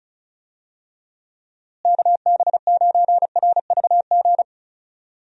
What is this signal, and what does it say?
Morse code, heard as a keyed tone, about 36 wpm, 700 Hz. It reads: KB9RVG
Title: USA Call Signs - 6 Characters - Repeated 2x - 35wpm